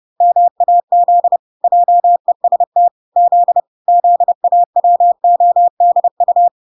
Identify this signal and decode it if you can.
Morse code, heard as a keyed tone, about 30 wpm, 700 Hz. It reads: MAZ JEST Z ZAWODU